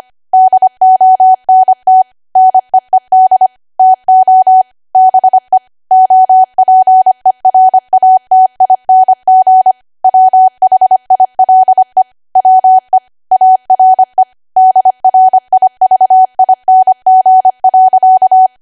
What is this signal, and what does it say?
Morse code, heard as a keyed tone, about 25 wpm, 750 Hz. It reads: DONT NEED TO BE OPERATING WHILE WE ARE DRIVING.
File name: W1AW-Level3/140218 25WPM